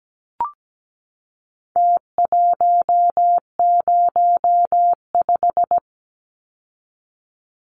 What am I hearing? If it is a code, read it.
Morse code, heard as a keyed tone, about 17 wpm, 700 Hz. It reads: T105
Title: Numbers - Contest Style - 4-Digit 17wpm